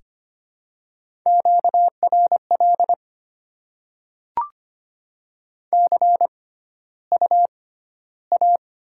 Morse code, heard as a keyed tone, about 25 wpm, spaced as for 13 wpm, 700 Hz. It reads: QRL CUA
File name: Top 20 QSO Elements - 6x Character Spacing - 25wpm